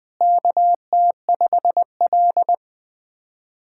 Morse code, heard as a keyed tone, about 23 wpm, 700 Hz. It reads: KT5L
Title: SOTA QSO Practice 20wpm